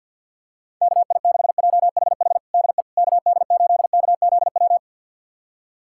Morse code, heard as a keyed tone, about 51 wpm, 700 Hz. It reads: KI6JHH DE KD8KZW
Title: USA Callsign DE Callsign 50wpm